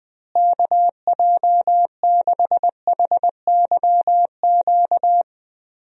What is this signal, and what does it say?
Morse code, heard as a keyed tone, about 20 wpm, 700 Hz. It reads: KJ6HYQ
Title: USA Call Signs - 6 Characters - Repeated 2x - 20wpm